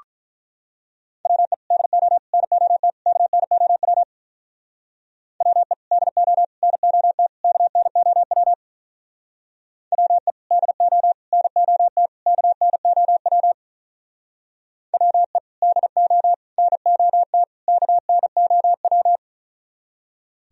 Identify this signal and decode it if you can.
Morse code, heard as a keyed tone, about 42 wpm, 700 Hz. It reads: WE DO NOT KNOW WE DO NOT KNOW WE DO NOT KNOW WE DO NOT KNOW
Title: Speed-Racing - Sets of 4 Words 35wpm